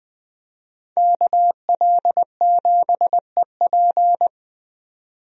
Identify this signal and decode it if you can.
Morse code, heard as a keyed tone, about 20 wpm, 700 Hz. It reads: KL7EP
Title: DXCC-100 Call Signs - 5 Characters - 20wpm